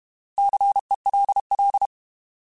Morse code, heard as a keyed tone, about 31 wpm, 800 Hz. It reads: CELL
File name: Alphabetical / Alphabetical mp3s / Cs @30